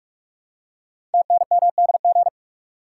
Morse code, heard as a keyed tone, about 44 wpm, 700 Hz. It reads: TNMDG
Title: Code Groups - 5 Letters - 45wpm